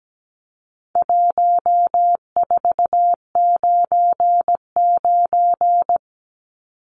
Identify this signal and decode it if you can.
Morse code, heard as a keyed tone, about 17 wpm, 700 Hz. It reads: 1499
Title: Numbers - 4-Digit 17wpm